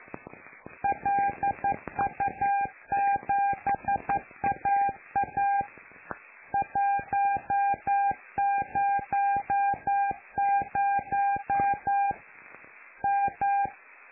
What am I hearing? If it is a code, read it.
Morse code, heard as a keyed tone, about 11 wpm, 800 Hz. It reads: LU7AA 100 M
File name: lu4aao / Globo Amsat